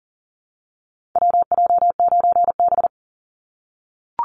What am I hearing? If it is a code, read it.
Morse code, heard as a keyed tone, about 40 wpm, 700 Hz. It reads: WJ9B
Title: Call Signs - 40wpm